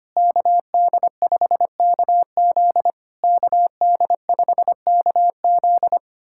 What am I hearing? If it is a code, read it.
Morse code, heard as a keyed tone, about 25 wpm, 700 Hz. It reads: KD5KZ KD5KZ
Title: SOTA QSO Practice 25wpm